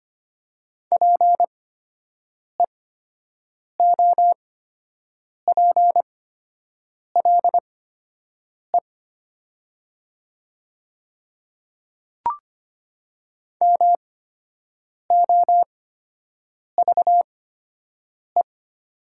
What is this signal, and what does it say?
Morse code, heard as a keyed tone, about 25 wpm, spaced as for 7 wpm, 700 Hz. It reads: PEOPLE MOVE
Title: 8x Character Spacing - Top 300 Words - 25wpm